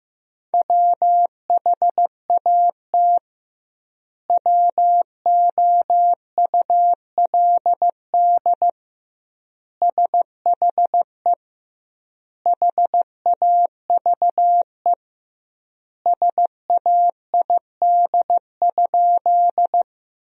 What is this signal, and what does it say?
Morse code, heard as a keyed tone, about 16 wpm, 700 Hz. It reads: WHAT WOULD SHE HAVE SAID?
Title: 2x Word-Spacing - Sentences from Top 100 Words - 15wpm